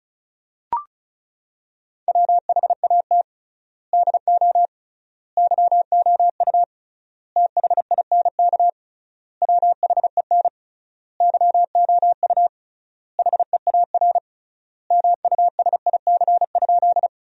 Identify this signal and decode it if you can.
Morse code, heard as a keyed tone, about 36 wpm, 700 Hz. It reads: WHAT DO YOU THINK WHEN YOU HEAR MUSIC?
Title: 3x Word-Spacing - Sentences from Top 300 Words - 35wpm